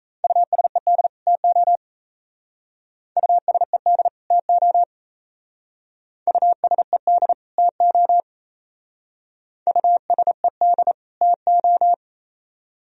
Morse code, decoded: USED TO USED TO USED TO USED TO

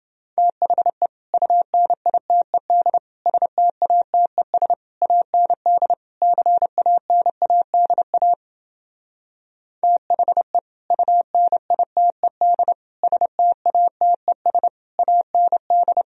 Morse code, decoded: THE UNITED STATES AND CANADA THE UNITED STATES AND

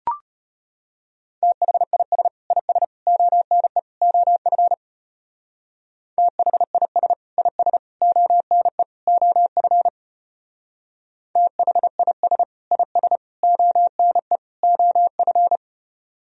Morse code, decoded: THIS IS ONE OF THIS IS ONE OF THIS IS ONE OF

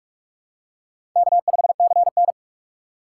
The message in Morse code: K5CN

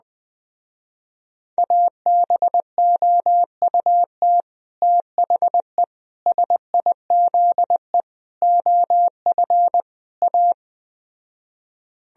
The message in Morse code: ABOUT THE SIZE OF A

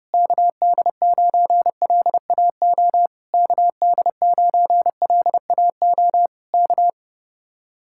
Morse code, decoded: KD9LAO KD9LAO K

